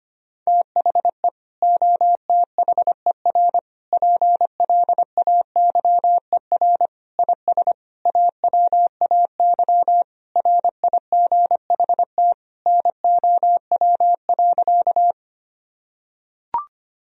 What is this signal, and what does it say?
Morse code, heard as a keyed tone, about 25 wpm, 700 Hz. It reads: THE OTHER PLAYER IS AWAY RIGHT NOW.
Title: Sentences from Top 1000 Words - Review Part 4 - 25wpm